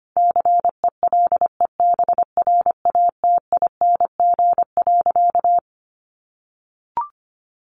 Morse code, heard as a keyed tone, about 25 wpm, 700 Hz. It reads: CELEBRATING.
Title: A Sound Of Thunder 25wpm